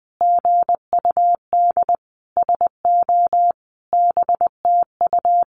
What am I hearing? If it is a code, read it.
Morse code, heard as a keyed tone, about 20 wpm, 700 Hz. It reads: GUD SO BTU
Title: Common QSO Phrases 20wpm